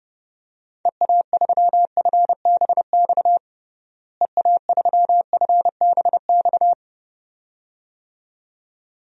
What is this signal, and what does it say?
Morse code, heard as a keyed tone, about 30 wpm, 700 Hz. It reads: EA3FBX EA3FBX
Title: DXCC-100 Call Signs - 6 Characters - Repeated 2x - 30wpm